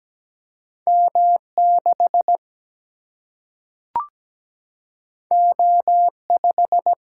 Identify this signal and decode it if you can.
Morse code, heard as a keyed tone, about 17 wpm, 700 Hz. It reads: M6 O5
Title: Single-Letter Single-Number 17wpm